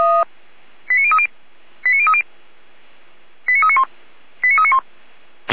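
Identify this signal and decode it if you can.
Five-tone selective call (ZVEI), heard as quick runs of five tones, tones of about 70 ms. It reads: 89030 89030 89391 89391
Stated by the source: ZVEI five-tone selective call received via SDR.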